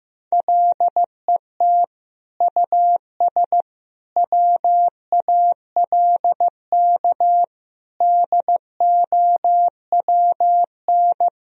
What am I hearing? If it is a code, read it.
Morse code, heard as a keyed tone, about 15 wpm, 700 Hz. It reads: LET US WALK DOWN
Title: Sentences from Top 1000 Words - Review Part 3 - 15wpm